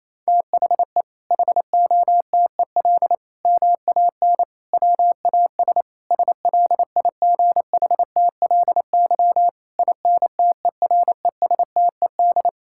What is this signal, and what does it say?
Morse code, heard as a keyed tone, about 28 wpm, 700 Hz. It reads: THE HOTEL MAN WAS SLIGHTLY INTERESTED
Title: Sentences from Top 1600 Words 28wpm